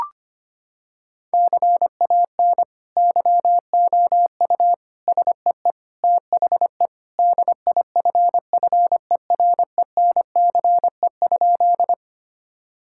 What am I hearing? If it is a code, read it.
Morse code, heard as a keyed tone, about 25 wpm, 700 Hz. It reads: CAN YOU SEE THE DIFFERENCE?
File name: practice / 25wpm / Sentences with 2-Word Phrases - 25wpm